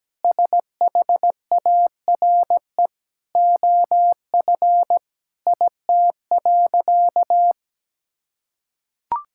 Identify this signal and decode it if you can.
Morse code, heard as a keyed tone, about 17 wpm, 700 Hz. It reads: SHARE OF IT.
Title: Sentences from Top 1000 Words - Review Part 6 - 17wpm